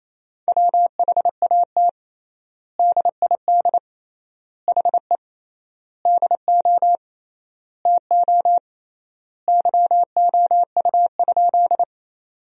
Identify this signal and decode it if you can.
Morse code, heard as a keyed tone, about 28 wpm, 700 Hz. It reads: WHAT DID HE DO TO YOU?
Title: 3x Word-Spacing - Sentences from Top 100 Words - 28wpm